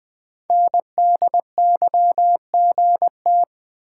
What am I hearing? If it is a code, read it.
Morse code, heard as a keyed tone, about 20 wpm, 700 Hz. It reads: NDYGT
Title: Code Groups - 5 Letter:Numbers - 20wpm